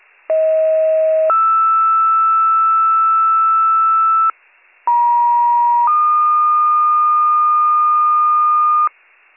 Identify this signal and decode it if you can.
Two-tone page, heard as a one-second tone, then a three-second tone, about 640 then 1320 Hz, about 950 then 1220 Hz. It is heard from 0.3 to 8.9 s.